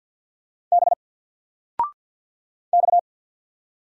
Morse code, heard as a keyed tone, about 48 wpm, 700 Hz. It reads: B X
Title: Single Letter-Number 50wpm